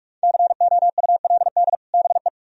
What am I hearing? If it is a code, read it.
Morse code, heard as a keyed tone, about 44 wpm, 700 Hz. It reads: COULD BE